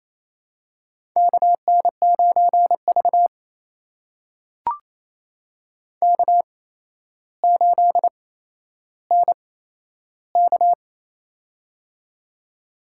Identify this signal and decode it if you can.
Morse code, heard as a keyed tone, about 28 wpm, spaced as for 12 wpm, 700 Hz. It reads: KN9V K8NK
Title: USA Call Signs - 8x Character Spacing - 28wpm